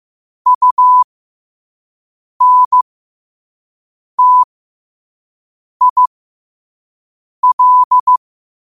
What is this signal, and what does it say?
Morse code, heard as a keyed tone, about 15 wpm, spaced as for 5 wpm, 1000 Hz. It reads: UNTIL
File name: qotdfiles / 5wpm